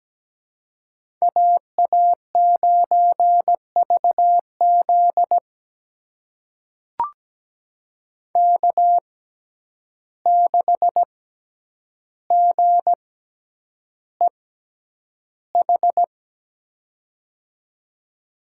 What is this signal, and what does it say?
Morse code, heard as a keyed tone, about 17 wpm, spaced as for 9 wpm, 700 Hz. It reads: AA9VZ K6GEH
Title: USA Call Signs - 5 Characters - 6x Character Spacing - 17wpm